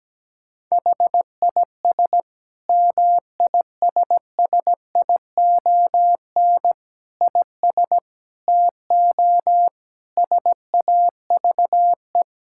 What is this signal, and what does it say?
Morse code, heard as a keyed tone, about 17 wpm, 700 Hz. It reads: HIS MISSION IS TO SAVE
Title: Sentences from Top 1600 Words 17wpm